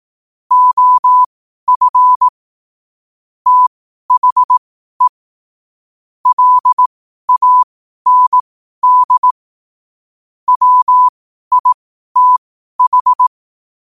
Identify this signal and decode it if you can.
Morse code, heard as a keyed tone, about 18 wpm, spaced as for 12 wpm, 1000 Hz. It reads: OF THE LAND WITH
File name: qotdfiles / 12wpm